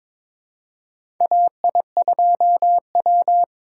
Morse code, heard as a keyed tone, about 22 wpm, 700 Hz. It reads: AI2W
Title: USA Call Signs - 4 Characters - 4x Character Spacing - 22wpm